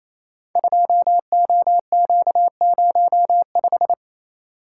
Morse code, heard as a keyed tone, about 27 wpm, 700 Hz. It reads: 2OQ05
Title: Code Groups - 5 Letter:Numbers - 28wpm